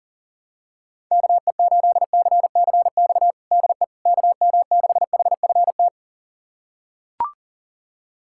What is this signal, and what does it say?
Morse code, heard as a keyed tone, about 40 wpm, 700 Hz. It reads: KE8CCX DE KM6HFT
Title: USA Callsign DE Callsign - 6 Characters - 40wpm